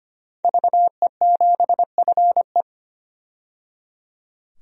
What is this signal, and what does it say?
Morse code, heard as a keyed tone, about 25 wpm, 700 Hz. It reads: VE7FE
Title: DXCC-100 Call Signs - 5 Characters - 25wpm